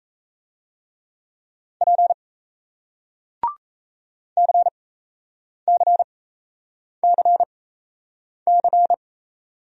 Morse code, heard as a keyed tone, about 38 wpm, spaced as for 8 wpm, 700 Hz. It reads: P CCCC